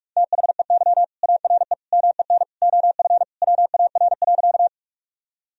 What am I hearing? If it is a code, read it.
Morse code, heard as a keyed tone, about 46 wpm, 700 Hz. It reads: THEY ARE MEN OF WAR.
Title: Sentences from Top 1000 Words - Review Part 2 - 45wpm